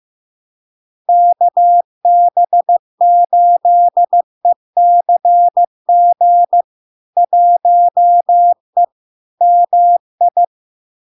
Morse code, heard as a keyed tone, about 15 wpm, 700 Hz. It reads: KB8ECG 1E MI
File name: LARK-FD-2022-Log-in-CW-15wpm-700-Hz